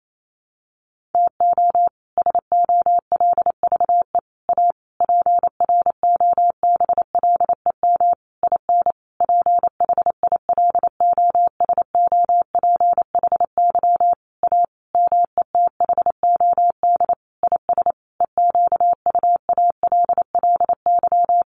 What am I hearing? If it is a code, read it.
Morse code, heard as a keyed tone, about 28 wpm, 700 Hz. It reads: TO SOLVE A PROBLEM IN PHILOSOPHY A METHOD IS EQUALLY
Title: Sentences from Top 1400 Words 28wpm